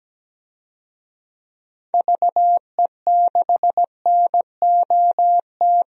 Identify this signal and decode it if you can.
Morse code, heard as a keyed tone, about 17 wpm, 700 Hz. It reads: VE6NOT
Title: DXCC-100 Call Signs - 6 Characters - 17wpm